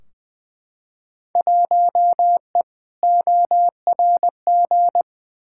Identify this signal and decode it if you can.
Morse code, heard as a keyed tone, about 20 wpm, 700 Hz. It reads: 1E ORG
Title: Field Day Exchanges - 20wpm